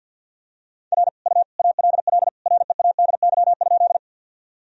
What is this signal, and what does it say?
Morse code, heard as a keyed tone, about 50 wpm, 700 Hz. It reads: R U ALL READY?
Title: Sentences from Top 400 Words and Abbreviations 50wpm